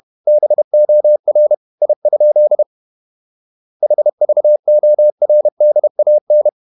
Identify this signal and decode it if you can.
Morse code, heard as a keyed tone, about 31 wpm, 600 Hz. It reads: BOR I? HVORDAN
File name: Ferie QRQ 31wpm